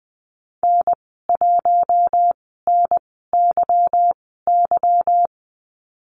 Morse code, heard as a keyed tone, about 20 wpm, spaced as for 14 wpm, 700 Hz. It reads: N1NYY
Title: 173 Lesson 28g - Number 9 - Call Signs 20wpm